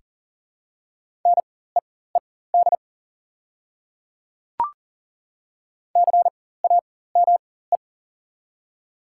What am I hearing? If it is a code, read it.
Morse code, heard as a keyed tone, about 40 wpm, spaced as for 19 wpm, 700 Hz. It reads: NEED CAME
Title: 4x Character Spacing - Top 300 Words - 40wpm